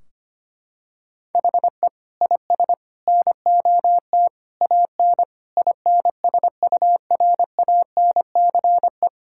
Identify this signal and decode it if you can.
Morse code, heard as a keyed tone, about 25 wpm, 700 Hz. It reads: HE IS NOT AN INSURANCE
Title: Sentences from Top 1600 Words 25wpm